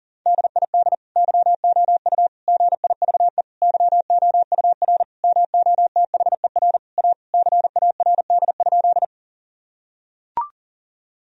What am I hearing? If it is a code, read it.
Morse code, heard as a keyed tone, about 40 wpm, 700 Hz. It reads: DID YOU GIVE YOUR MOTHER A CARD?
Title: Sentences from Top 1000 Words - Review Part 4 - 40wpm